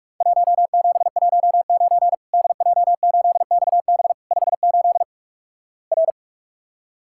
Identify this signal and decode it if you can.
Morse code, heard as a keyed tone, about 46 wpm, 700 Hz. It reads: 1719 DJ8XB 58 R
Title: Worked All Europe DX Contest - QTC Exchange - 45wpm